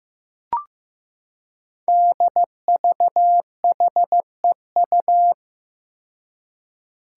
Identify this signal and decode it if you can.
Morse code, heard as a keyed tone, about 15 wpm, 700 Hz. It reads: DVHEU
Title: Five Challenging Letter-Number Combinations 15wpm